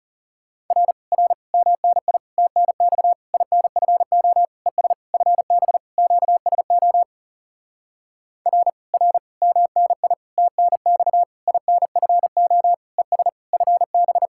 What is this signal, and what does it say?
Morse code, heard as a keyed tone, about 37 wpm, 700 Hz. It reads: R R MNI TNX INFO ES FB QSO R R MNI TNX INFO ES FB
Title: Speed-Racing Common QSO Phrases - 30wpm